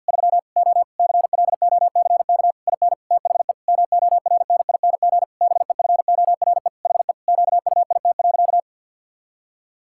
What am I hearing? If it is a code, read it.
Morse code, heard as a keyed tone, about 50 wpm, 700 Hz. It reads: 3 O CLOCK IN THE MORNING BEFORE HE QUIT.
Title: Pangram Challenge v2 - 50wpm